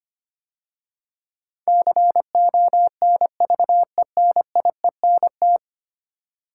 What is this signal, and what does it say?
Morse code, heard as a keyed tone, about 25 wpm, 700 Hz. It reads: CONVENIENT
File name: 10 Letter Words - First 3 Letters Given - 25wpm